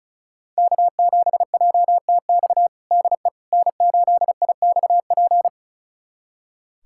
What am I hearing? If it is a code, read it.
Morse code, heard as a keyed tone, about 36 wpm, 700 Hz. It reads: K7JTX DE N8IXP
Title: USA Callsign DE Callsign - 5 Characters - 35wpm